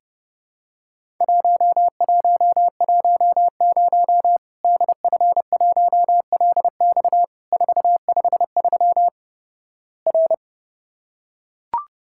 Morse code, heard as a keyed tone, about 31 wpm, 700 Hz. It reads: 1110 DF1LX 453 R
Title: Worked All Europe DX Contest - QTC Exchange - 30wpm